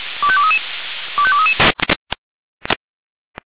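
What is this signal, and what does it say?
Five-tone selective call (ZVEI), heard as quick runs of five tones, tones of about 70 ms. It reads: 25233 25233
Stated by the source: ZVEI five-tone selective call 25431 received by SDR.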